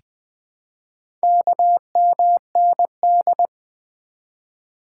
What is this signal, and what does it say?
Morse code, heard as a keyed tone, about 20 wpm, 700 Hz. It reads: KMND
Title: Four Letter-Number Combinations - 20wpm